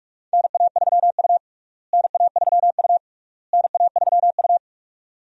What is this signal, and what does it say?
Morse code, heard as a keyed tone, about 42 wpm, 700 Hz. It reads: NA3U NA3U NA3U
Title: USA Call Signs - 4 Characters - Repeated 3x - 45wpm